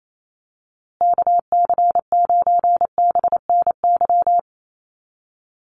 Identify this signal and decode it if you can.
Morse code, heard as a keyed tone, about 28 wpm, 700 Hz. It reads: KC9BNY
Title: USA Call Signs Repeated 2x 28wpm